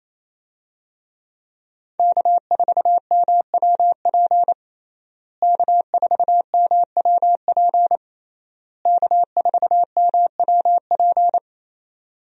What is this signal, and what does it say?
Morse code, heard as a keyed tone, about 28 wpm, 700 Hz. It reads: K4MWP K4MWP K4MWP
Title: USA Call Signs - 5 Characters - Repeated 3x - 28wpm